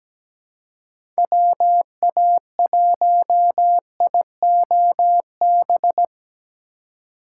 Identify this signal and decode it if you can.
Morse code, heard as a keyed tone, about 17 wpm, 700 Hz. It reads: WA1IOB
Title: USA Call Signs - 6 Characters - 4x Character Spacing - 17wpm